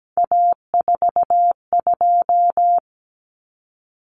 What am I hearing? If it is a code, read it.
Morse code, heard as a keyed tone, about 17 wpm, 690 Hz. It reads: A42
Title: Common Cut Numbers - 3-Digit 17wpm